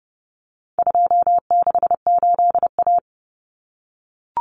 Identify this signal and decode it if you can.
Morse code, heard as a keyed tone, about 30 wpm, 700 Hz. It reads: 268A
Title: Common Cut Numbers - 4-Digit 30wpm